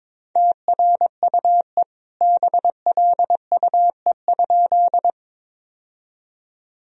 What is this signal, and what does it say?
Morse code, heard as a keyed tone, about 22 wpm, 700 Hz. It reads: TRUE BLUE?
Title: Sentences from Top 1000 Words - Review Part 3 - 22wpm